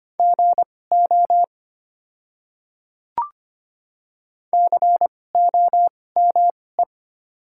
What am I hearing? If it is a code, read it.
Morse code, heard as a keyed tone, about 25 wpm, spaced as for 18 wpm, 690 Hz. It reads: GO COME
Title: 2x Character Spacing - Top 200 Words - 25wpm